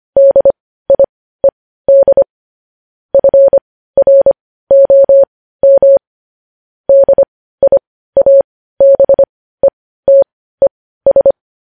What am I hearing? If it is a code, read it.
Morse code, heard as a keyed tone, about 25 wpm, spaced as for 15 wpm, 550 Hz. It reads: DIED FROM DIABETES
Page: news-headlines-in-morse-code-at-15-wpm